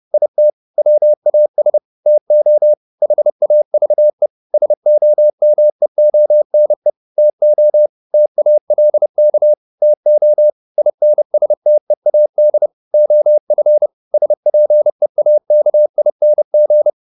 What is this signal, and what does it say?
Morse code, heard as a keyed tone, about 30 wpm, 600 Hz. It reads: IT WAS TO HAVE SOMEONE TO TALK TO INSTEAD OF SPEAKING